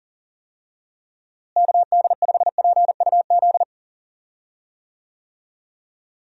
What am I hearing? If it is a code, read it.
Morse code, heard as a keyed tone, about 37 wpm, 700 Hz. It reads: KD5PUZ
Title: USA Call Signs - 6 Characters - Repeated 2x - 40wpm